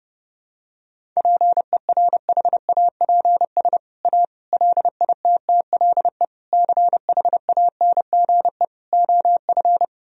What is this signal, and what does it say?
Morse code, heard as a keyed tone, about 30 wpm, 700 Hz. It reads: PERHAPS A LITTLE CHANGE OF
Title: Sentences from Top 1000 Words - Review Part 1 - 30wpm